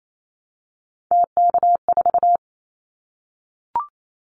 Morse code, heard as a keyed tone, about 28 wpm, 700 Hz. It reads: TK4
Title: Callsign Prefixes for Top 100 DXCC - 28wpm